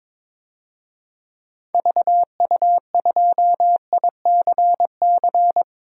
Morse code, heard as a keyed tone, about 22 wpm, 700 Hz. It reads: VU2ICC
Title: DXCC-100 Call Signs - 6 Characters - 22wpm